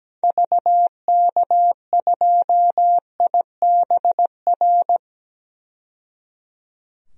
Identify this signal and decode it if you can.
Morse code, heard as a keyed tone, about 17 wpm, 700 Hz. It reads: VK2IBR
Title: DXCC-100 Call Signs - 6 Characters - 17wpm